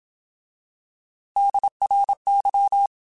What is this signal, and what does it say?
Morse code, heard as a keyed tone, about 27 wpm, 800 Hz. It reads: DRY
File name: Alphabetical / Alphabetical mp3s / Ds @25